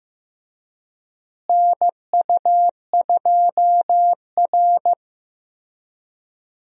Morse code, heard as a keyed tone, about 15 wpm, 700 Hz. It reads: NU2R
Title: USA Call Signs - 4 Characters - Repeated 3x - 15wpm